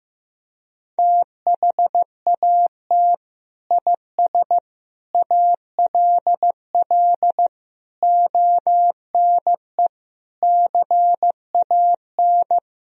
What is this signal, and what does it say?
Morse code, heard as a keyed tone, about 16 wpm, 700 Hz. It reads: THAT IS ALL ONE CAN
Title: Sentences from Top 200 Words and Abbreviations 15wpm